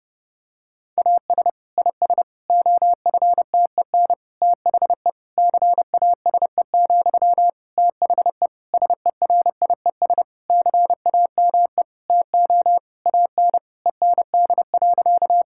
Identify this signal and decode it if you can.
Morse code, heard as a keyed tone, about 32 wpm, 700 Hz. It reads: AS IS OFTEN THE CASE, THE SERIES CAME TO AN END.
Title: Sentences from Top 1000 Words - Review Part 5 - 30wpm